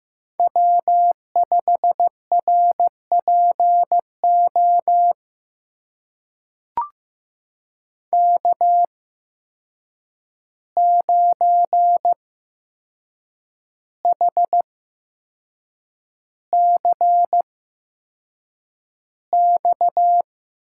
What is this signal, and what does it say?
Morse code, heard as a keyed tone, about 15 wpm, spaced as for 6 wpm, 700 Hz. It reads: W5RPO K9HCX
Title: USA Call Signs - 5 Characters - 8x Character Spacing - 15wpm